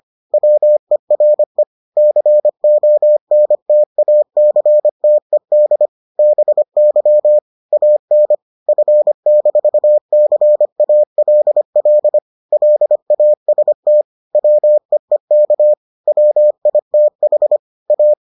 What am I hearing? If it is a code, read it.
Morse code, heard as a keyed tone, about 25 wpm, 600 Hz. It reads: WERE CONTACTED BY AN F-CALL LAST WEEK WITH A